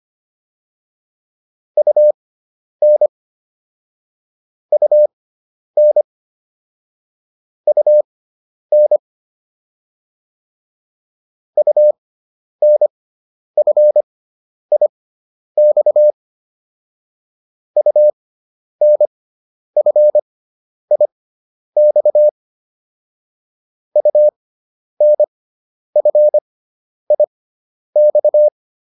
Morse code, heard as a keyed tone, about 25 wpm, spaced as for 10 wpm, 600 Hz. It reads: UN UN UN UNFIX UNFIX UNFIX